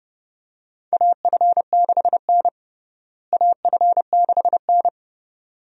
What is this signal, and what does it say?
Morse code, heard as a keyed tone, about 30 wpm, 700 Hz. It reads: AF6N AF6N
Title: USA Call Signs - 4 Characters - Repeated 2x - 30wpm